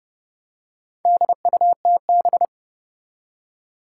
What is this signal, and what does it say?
Morse code, heard as a keyed tone, about 30 wpm, 700 Hz. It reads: DUTB
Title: Four Challenging Letter-Number Combinations 30wpm